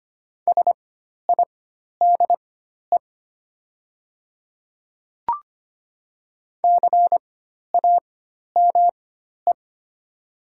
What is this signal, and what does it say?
Morse code, heard as a keyed tone, about 25 wpm, spaced as for 12 wpm, 700 Hz. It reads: SIDE CAME
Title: 4x Character Spacing - Top 200 Words - 25wpm